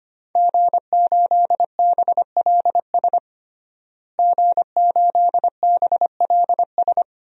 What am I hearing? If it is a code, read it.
Morse code, heard as a keyed tone, about 25 wpm, 700 Hz. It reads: G8BLS G8BLS
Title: DXCC-100 Call Signs - 5 Characters - Repeated 2x - 25wpm